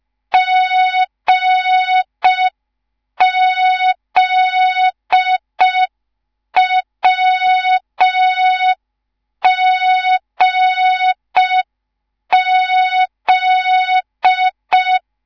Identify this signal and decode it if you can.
Morse code, heard as a keyed tone, about 5 wpm, 3700 Hz. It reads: GZWGZ